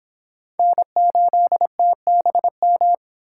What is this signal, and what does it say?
Morse code, heard as a keyed tone, about 25 wpm, 700 Hz. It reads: N8TBM